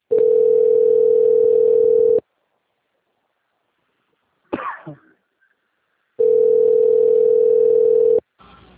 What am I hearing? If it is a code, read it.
Ringback tone (North America), 440 + 480 Hz.
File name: out- - - - - .21178